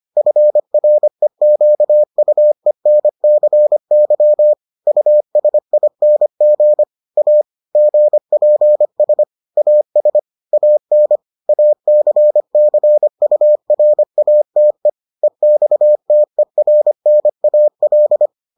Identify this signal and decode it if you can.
Morse code, heard as a keyed tone, about 25 wpm, 600 Hz. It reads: FREQUENCY USING A GPS AS AN ACCURATE EXTERNAL